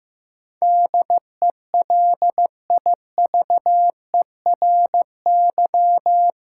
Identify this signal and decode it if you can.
Morse code, heard as a keyed tone, about 15 wpm, 700 Hz. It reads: DELIVERY
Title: Sentences from Top 1500 Words - New Word Drill - 15wpm